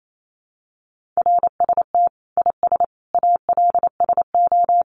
Morse code, heard as a keyed tone, about 28 wpm, 700 Hz. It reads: RST IS ALSO
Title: Common QSO Phrases 28wpm